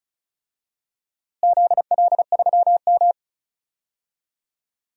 Morse code, heard as a keyed tone, about 36 wpm, 700 Hz. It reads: ZL3M
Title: DXCC-100 Call Signs - 4 Characters - Repeated 2x - 35wpm